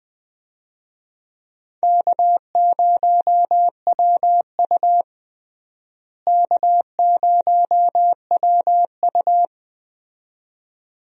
Morse code, heard as a keyed tone, about 20 wpm, 700 Hz. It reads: K0WU K0WU
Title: USA Call Signs - 4 Characters - Repeated 2x - 20wpm